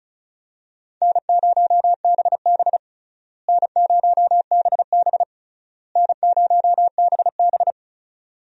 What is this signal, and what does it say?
Morse code, heard as a keyed tone, about 34 wpm, 700 Hz. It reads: N0BB N0BB N0BB
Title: USA Call Signs - 4 Characters - Repeated 3x - 35wpm